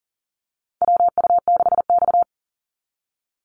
Morse code, heard as a keyed tone, about 40 wpm, 700 Hz. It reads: WU6X
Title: Call Signs - 40wpm